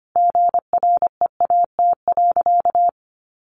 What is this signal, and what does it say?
Morse code, heard as a keyed tone, about 25 wpm, 700 Hz. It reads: GREAT.
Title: Sentences from Top 1200 Words 25wpm